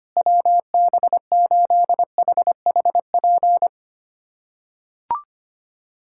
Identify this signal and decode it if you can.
Morse code, heard as a keyed tone, about 25 wpm, 700 Hz. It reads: WB8HHP
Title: USA Callsign DE Callsign - 6 Characters - 25wpm